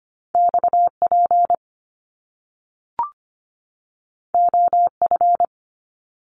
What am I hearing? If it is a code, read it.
Morse code, heard as a keyed tone, about 25 wpm, 700 Hz. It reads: XP OF
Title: Top Two Letter Combinations in Words 25wpm